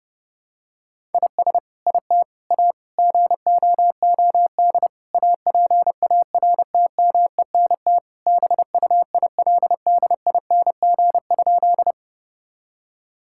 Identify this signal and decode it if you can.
Morse code, heard as a keyed tone, about 32 wpm, 700 Hz. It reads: IS IT A GOOD APARTMENT BUILDING?
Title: Sentences from Top 1000 Words - Review Part 6 - 30wpm